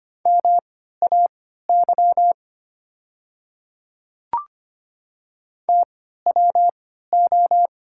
Morse code, heard as a keyed tone, about 25 wpm, spaced as for 14 wpm, 700 Hz. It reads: MAY TWO
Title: 3x Character Spacing - Top 100 Words - 25wpm